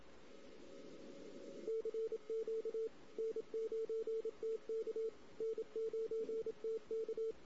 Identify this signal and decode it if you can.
Morse code, heard as a keyed tone, about 26 wpm, 450 Hz. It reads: CQ N9TK N9TK
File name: CW / 2008-ss-cw-k5tr-op-n5rz -16:00